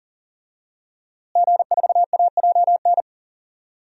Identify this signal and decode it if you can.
Morse code, heard as a keyed tone, about 40 wpm, 700 Hz. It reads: G4AJN
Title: DXCC-100 Call Signs - 5 Characters - Repeated 2x - 40wpm